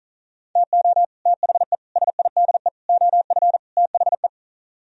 Morse code, heard as a keyed tone, about 43 wpm, 700 Hz. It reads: TO THE SIDE OF THE
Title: Speed-Racing - Sets of 5 Words 35wpm